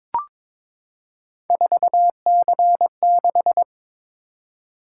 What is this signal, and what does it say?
Morse code, heard as a keyed tone, about 22 wpm, 700 Hz. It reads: 4C6
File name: practice / 22wpm / Three Letter-Number Combinations - 22wpm